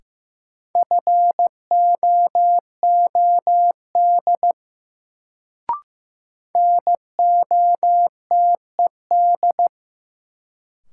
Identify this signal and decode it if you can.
Morse code, heard as a keyed tone, about 15 wpm, 700 Hz. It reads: FOOD NOTED
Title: Sets of 5 Words - New Word Drill - 15wpm